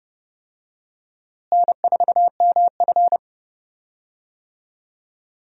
Morse code, heard as a keyed tone, about 30 wpm, 700 Hz. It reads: N4MF
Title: USA Call Signs - 4 Characters - Repeated 3x - 30wpm